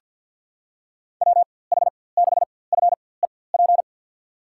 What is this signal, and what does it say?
Morse code, heard as a keyed tone, about 47 wpm, spaced as for 23 wpm, 700 Hz. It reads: WH6FEP
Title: USA Call Signs - 6 Characters - 4x Character Spacing - 50wpm